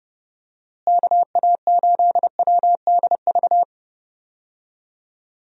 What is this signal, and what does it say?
Morse code, heard as a keyed tone, about 30 wpm, 700 Hz. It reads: KA8WDV
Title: USA Call Signs - 6 Characters - Repeated 3x - 30wpm